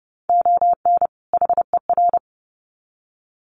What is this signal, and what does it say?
Morse code, heard as a keyed tone, about 30 wpm, 700 Hz. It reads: ON HER
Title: Sets of 2 Words 30wpm